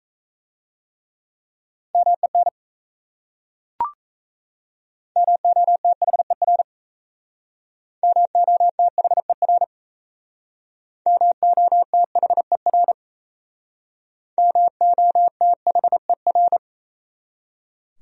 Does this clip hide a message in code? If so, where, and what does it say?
Yes: Morse code from 1.9 to 16.6 s, heard as a keyed tone, about 37 wpm, 700 Hz. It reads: MEN MOTHER MOTHER MOTHER MOTHER